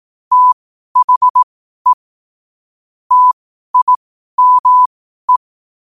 Morse code, heard as a keyed tone, about 18 wpm, spaced as for 12 wpm, 1000 Hz. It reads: THE TIME
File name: qotdfiles / 12wpm